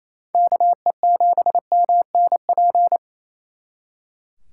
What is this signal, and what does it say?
Morse code, heard as a keyed tone, about 36 wpm, 700 Hz. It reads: KE7MNP